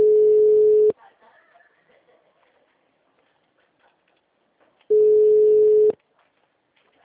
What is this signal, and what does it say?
Ringback tone (Europe, 425 Hz), with speech.